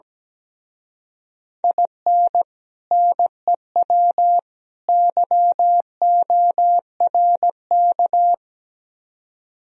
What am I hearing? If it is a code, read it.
Morse code, heard as a keyed tone, about 17 wpm, 700 Hz. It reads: IN NEW YORK
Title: Speed-Racing - Sets of 3 Words 17wpm